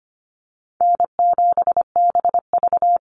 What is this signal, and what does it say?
Morse code, heard as a keyed tone, about 25 wpm, 700 Hz. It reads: N7BV
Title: Call Signs - 25wpm